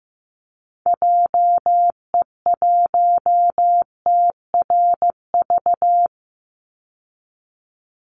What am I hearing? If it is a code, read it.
Morse code, heard as a keyed tone, about 15 wpm, 700 Hz. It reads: JE1TRV
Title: Call Signs Repeated 2x - 15wpm